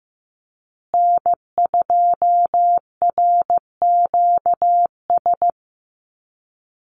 Morse code, heard as a keyed tone, about 15 wpm, 700 Hz. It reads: N2RQS
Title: USA Call Signs Repeated 2x 15wpm